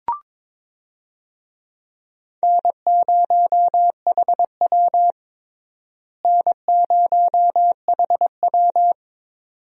Morse code, heard as a keyed tone, about 21 wpm, 700 Hz. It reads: N0HW N0HW
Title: USA Call Signs - 4 Characters - Repeated 2x - 22wpm